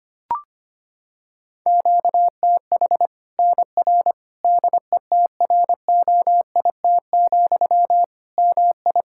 Morse code, heard as a keyed tone, about 25 wpm, 700 Hz. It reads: QTH NR DETROIT, MI
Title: Practice QSO - Exchange 1 - Beginning - 25wpm